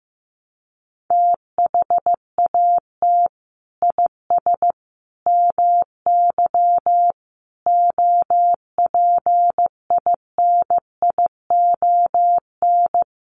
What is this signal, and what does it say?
Morse code, heard as a keyed tone, about 15 wpm, 700 Hz. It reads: THAT IS MY OPINION
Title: Sentences from Top 1200 Words 15wpm